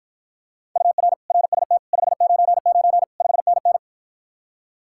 Morse code, heard as a keyed tone, about 53 wpm, 700 Hz. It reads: UR RST 599 5NN